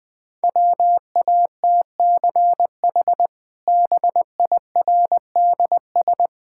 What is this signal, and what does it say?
Morse code, heard as a keyed tone, about 19 wpm, 700 Hz. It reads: WATCH BIRDS
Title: Easy Sentences with Hints - 20wpm